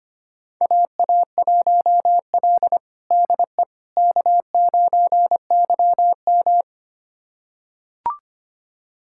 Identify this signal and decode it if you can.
Morse code, heard as a keyed tone, about 25 wpm, 700 Hz. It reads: AA1L DE K9YM
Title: USA Callsign DE Callsign - 4 Characters - 25wpm